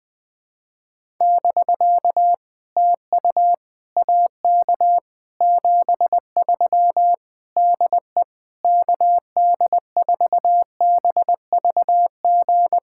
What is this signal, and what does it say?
Morse code, heard as a keyed tone, about 20 wpm, 700 Hz. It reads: <BK> TU AK 73 DE KD4BVG
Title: POTA QSO Practice - 20wpm